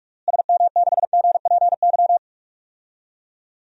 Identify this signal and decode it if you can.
Morse code, heard as a keyed tone, about 46 wpm, 700 Hz. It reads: SM6GPY
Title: DXCC-100 Call Signs - 6 Characters - Repeated 3x - 45wpm